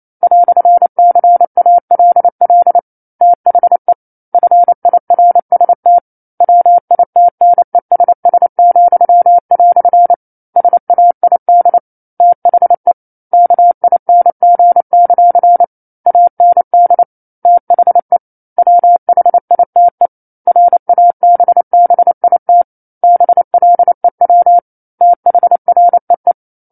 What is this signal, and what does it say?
Morse code, heard as a keyed tone, about 28 wpm, 700 Hz. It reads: "CALL THE FIRST WITNESS," SAID THE KING; AND THE WHITE RABBIT BLEW THREE
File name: 09-Chapter-09 28 WPM